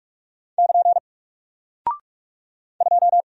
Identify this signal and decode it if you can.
Morse code, heard as a keyed tone, about 46 wpm, 700 Hz. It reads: <KN> 2
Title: Single Character (Letter:Number:Punctuation:Prosign) 45wpm